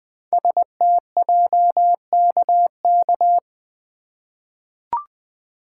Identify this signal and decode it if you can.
Morse code, heard as a keyed tone, about 20 wpm, 700 Hz. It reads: STJKK
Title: Code Groups - 5 Letters - 20wpm